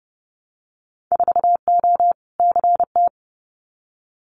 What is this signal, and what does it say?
Morse code, heard as a keyed tone, about 30 wpm, 700 Hz. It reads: 4O CT
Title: Winter Field Day Exchanges Encore 30wpm